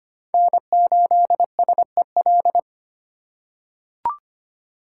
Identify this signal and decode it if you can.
Morse code, heard as a keyed tone, about 25 wpm, 700 Hz. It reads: N8SEL
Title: USA Call Signs - 5 Characters - 3x Character Spacing - 25wpm